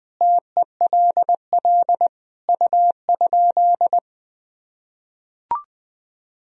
Morse code, heard as a keyed tone, about 20 wpm, 700 Hz. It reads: TELL U?
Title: Sentences from Top 200 Words and Abbreviations 20wpm